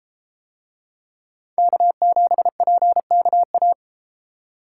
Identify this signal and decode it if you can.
Morse code, heard as a keyed tone, about 31 wpm, 700 Hz. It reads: K7PKA